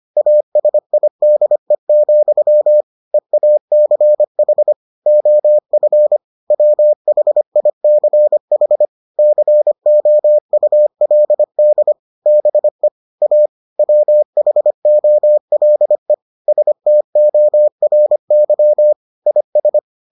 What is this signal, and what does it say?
Morse code, heard as a keyed tone, about 25 wpm, 600 Hz. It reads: ASIDE, EACH OF WHICH COULD BE A WHOLE STORY IS